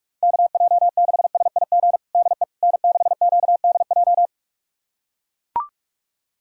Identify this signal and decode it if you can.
Morse code, heard as a keyed tone, about 41 wpm, 700 Hz. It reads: KJ6SIG DE N6QDJ
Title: USA Callsign DE Callsign 45wpm